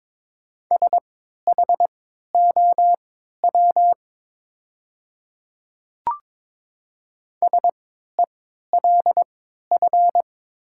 Morse code, heard as a keyed tone, about 21 wpm, spaced as for 12 wpm, 700 Hz. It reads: SHOW SELF
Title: 3x Character Spacing - Top 200 Words - 22wpm